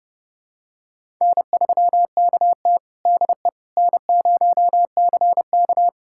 Morse code, heard as a keyed tone, about 29 wpm, 700 Hz. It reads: N3KT DE N0CK
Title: USA Callsign DE Callsign - 4 Characters - 30wpm